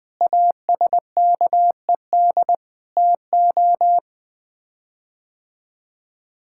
Morse code, heard as a keyed tone, about 21 wpm, 700 Hz. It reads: ASKED TO
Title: Sets of 6 Words - 20wpm